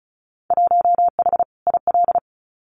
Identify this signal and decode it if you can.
Morse code, heard as a keyed tone, about 34 wpm, 700 Hz. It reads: 1H IL
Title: Winter Field Day Exchanges 35wpm